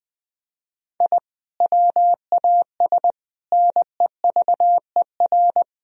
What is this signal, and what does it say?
Morse code, heard as a keyed tone, about 21 wpm, 700 Hz. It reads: I WAS NEVER
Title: Sentences from Top 1000 Words - Review Part 3 - 20wpm